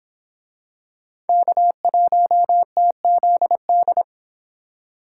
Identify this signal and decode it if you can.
Morse code, heard as a keyed tone, about 24 wpm, 700 Hz. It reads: K1TZD